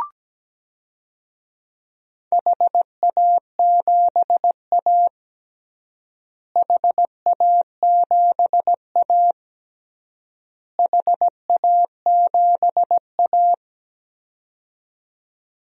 Morse code, heard as a keyed tone, about 17 wpm, 700 Hz. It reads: HA7A HA7A HA7A
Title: DXCC-100 Call Signs - 4 Characters - Repeated 3x - 17wpm